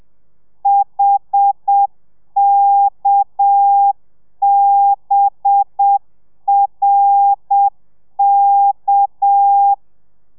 Morse code, heard as a keyed tone, about 7 wpm, 800 Hz. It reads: HKBRK